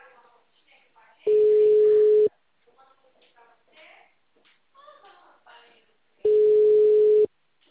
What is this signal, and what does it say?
Ringback tone (Europe, 425 Hz), with speech.